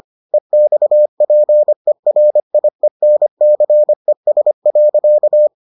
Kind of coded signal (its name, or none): Morse code